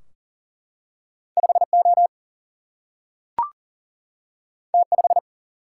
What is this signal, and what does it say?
Morse code, heard as a keyed tone, about 40 wpm, 700 Hz. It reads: 5O T5
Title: Single-Letter Single-Number 40wpm